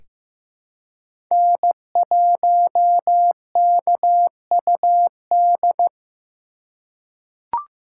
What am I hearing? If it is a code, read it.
Morse code, heard as a keyed tone, about 16 wpm, 700 Hz. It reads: N1KUD
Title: USA Call Signs - 5 Characters - 2x Character Spacing - 15wpm